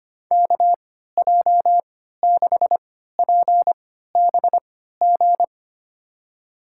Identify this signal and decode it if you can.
Morse code, heard as a keyed tone, about 25 wpm, spaced as for 14 wpm, 700 Hz. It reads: KJ6PBG
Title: USA Call Signs - 3x Character Spacing - 25wpm